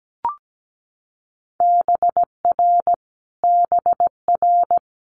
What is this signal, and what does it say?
Morse code, heard as a keyed tone, about 17 wpm, 700 Hz. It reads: BR BR
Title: Multi-letter Phonemes Repeated 3x 17wpm